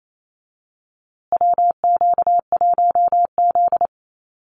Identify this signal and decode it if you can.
Morse code, heard as a keyed tone, about 28 wpm, 700 Hz. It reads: WQ1Z
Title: USA Call Signs Repeated 2x 28wpm